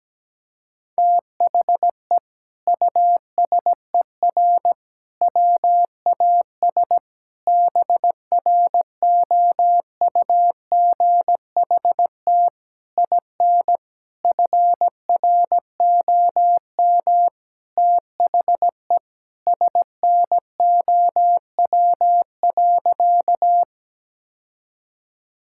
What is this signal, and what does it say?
Morse code, heard as a keyed tone, about 17 wpm, 700 Hz. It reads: THE USER WAS BROUGHT IN FROM THE SNOW.
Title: Sentences from Top 1000 Words - Review Part 5 - 17wpm